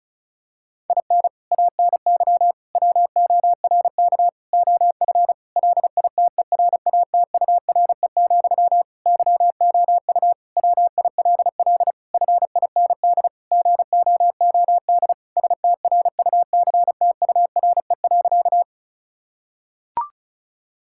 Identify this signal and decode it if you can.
Morse code, heard as a keyed tone, about 36 wpm, 700 Hz. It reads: IN ANY WORK OF LITERATURE, YOU WILL FIND GOOD STRUCTURE.
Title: Sentences from Top 1000 Words - Review Part 6 - 35wpm